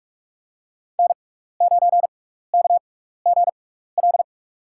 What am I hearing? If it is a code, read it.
Morse code, heard as a keyed tone, about 46 wpm, spaced as for 16 wpm, 700 Hz. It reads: N9KGL
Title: USA Call Signs - 5 Characters - 6x Character Spacing - 45wpm